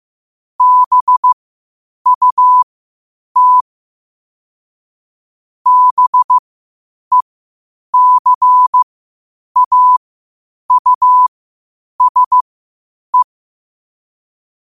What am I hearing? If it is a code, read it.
Morse code, heard as a keyed tone, about 15 wpm, spaced as for 8 wpm, 1000 Hz. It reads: BUT BECAUSE
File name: qotdfiles / 8wpm